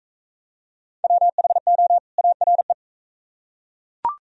Morse code, heard as a keyed tone, about 38 wpm, 700 Hz. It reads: WHO ARE